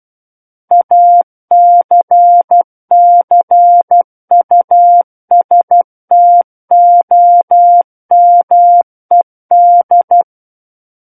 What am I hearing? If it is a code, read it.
Morse code, heard as a keyed tone, about 12 wpm, 700 Hz. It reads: ACCUSTOMED
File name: Book One / 03-Chapter 3 - 12 WPM